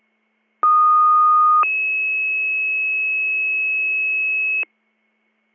Two-tone page, heard as a one-second tone, then a three-second tone, about 1230 then 2600 Hz, with noise. It is heard from 0.6 to 4.6 s.